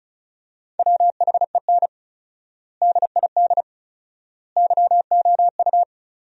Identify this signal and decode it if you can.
Morse code, heard as a keyed tone, about 36 wpm, 700 Hz. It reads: WHEN DID YOU
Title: 4x Word-Spacing - Sentences from Top 300 Words - 35wpm